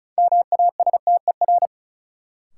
Morse code, heard as a keyed tone, about 36 wpm, 700 Hz. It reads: MASTER
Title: Sentences from Top 1300 Words - New Word Drill - 35wpm